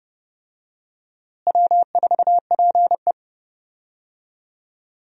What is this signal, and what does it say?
Morse code, heard as a keyed tone, about 30 wpm, 700 Hz. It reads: W4PE
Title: USA Call Signs - 4 Characters - Repeated 3x - 30wpm